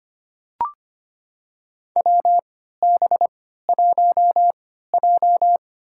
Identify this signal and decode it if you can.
Morse code, heard as a keyed tone, about 25 wpm, spaced as for 14 wpm, 700 Hz. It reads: WB1J
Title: USA Call Signs - 4 Characters - 3x Character Spacing - 25wpm